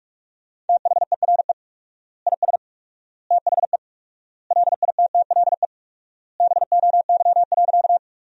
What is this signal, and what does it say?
Morse code, heard as a keyed tone, about 43 wpm, 700 Hz. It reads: THERE IS THE LITTLE BOY.
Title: 4x Word-Spacing - Sentences from Top 200 Words - 45wpm